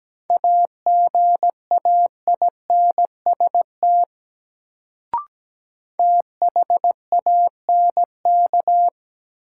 Morse code, heard as a keyed tone, about 17 wpm, 700 Hz. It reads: AGAINST THANK
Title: Sets of 2 Words - New Word Drill - 17wpm